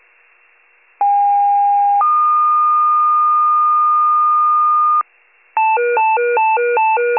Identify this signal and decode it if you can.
Two-tone page, heard as a one-second tone, then a three-second tone, about 810 then 1220 Hz. It is heard from 1.0 to 5.0 s.
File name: wcalert